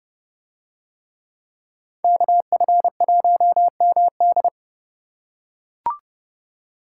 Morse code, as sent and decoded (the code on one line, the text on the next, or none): -.- ..-. .---- -- -..
KF1MD